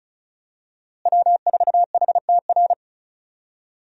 Morse code, heard as a keyed tone, about 34 wpm, 700 Hz. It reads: W4HTR
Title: USA Call Signs - 5 Characters - Repeated 3x - 35wpm